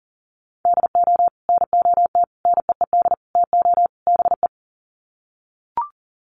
Morse code, heard as a keyed tone, about 40 wpm, 700 Hz. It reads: DO NOT NEED TO BE
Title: Sets of 5 Words Encore 40wpm